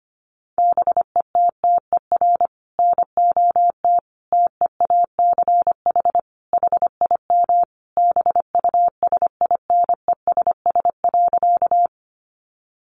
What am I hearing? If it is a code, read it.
Morse code, heard as a keyed tone, about 25 wpm, 700 Hz. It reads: BETTER NOT TEACH HIM BUSINESS.
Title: Sentences from Top 600 Words 25wpm